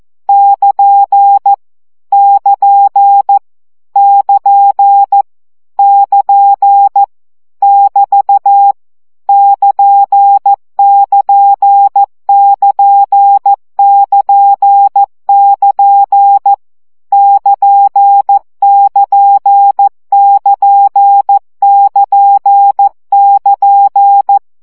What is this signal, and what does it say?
Morse code, heard as a keyed tone, about 14 wpm, 800 Hz. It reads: <KN> <KN> <KN> <KN> <BT> <KN><KN><KN><KN><KN> <KN><KN><KN><KN><KN>
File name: CW gyakorlas nyito 12 baud